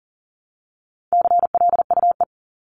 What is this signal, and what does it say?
Morse code, heard as a keyed tone, about 40 wpm, 700 Hz. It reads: CLUE
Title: Top 1751-2000 Words 40wpm